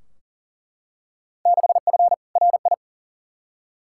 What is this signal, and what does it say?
Morse code, heard as a keyed tone, about 40 wpm, 700 Hz. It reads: 6F RI
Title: Field Day Exchanges Encore - 40wpm